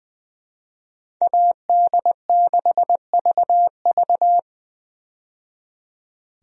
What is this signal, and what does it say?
Morse code, heard as a keyed tone, about 20 wpm, 700 Hz. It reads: AD6VV
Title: USA Call Signs - 5 Characters - 20wpm